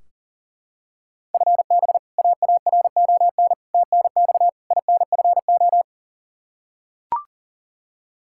Morse code, heard as a keyed tone, about 40 wpm, 700 Hz. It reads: FB AARON TNX INFO
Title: Practice QSO - Exchange 3 - More Info - 40wpm